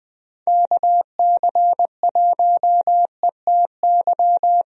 Morse code, heard as a keyed tone, about 20 wpm, 700 Hz. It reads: KC1ETY
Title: USA Call Signs - 6 Characters - Repeated 3x - 20wpm